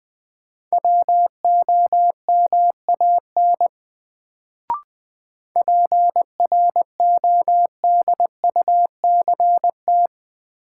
Morse code, heard as a keyed tone, about 20 wpm, 700 Hz. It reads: WOMAN PRODUCT
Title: Sentences from Top 1000 Words - Review Part 2 - New Word Drill - 20wpm